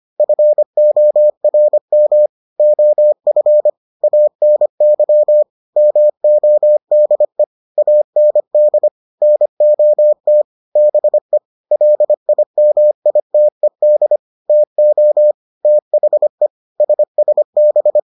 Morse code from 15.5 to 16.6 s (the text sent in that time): THE